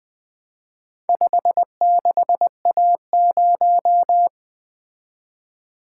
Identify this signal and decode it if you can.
Morse code, heard as a keyed tone, about 21 wpm, 700 Hz. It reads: 56A0
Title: Four Letter-Number Combinations - 20wpm